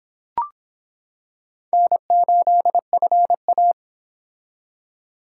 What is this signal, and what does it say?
Morse code, heard as a keyed tone, about 25 wpm, 700 Hz. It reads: N8FA